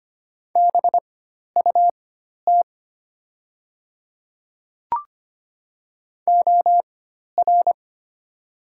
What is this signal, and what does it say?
Morse code, heard as a keyed tone, about 25 wpm, spaced as for 12 wpm, 700 Hz. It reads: BUT OR
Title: 4x Character Spacing - Top 100 Words - 25wpm